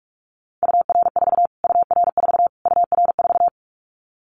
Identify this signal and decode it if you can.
Morse code, heard as a keyed tone, about 46 wpm, 700 Hz. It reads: UR4 UR4 UR4
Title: Callsign Prefixes for Top 25 DXCC - Repeated 3x - 45wpm